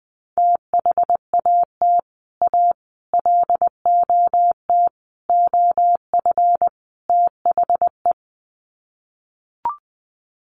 Morse code, heard as a keyed tone, about 20 wpm, 700 Hz. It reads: THAT A LOT OF THE
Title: Sets of 5 Words Encore2 20wpm